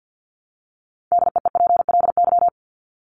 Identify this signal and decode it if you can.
Morse code, heard as a keyed tone, about 48 wpm, 700 Hz. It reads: DEEPLY
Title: LY Suffix - Top 100 Words with Base Words 50wpm